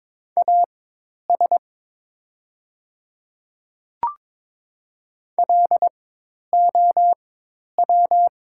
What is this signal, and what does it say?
Morse code, heard as a keyed tone, about 22 wpm, spaced as for 10 wpm, 700 Hz. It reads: AS LOW
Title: 4x Character Spacing - Top 200 Words - 22wpm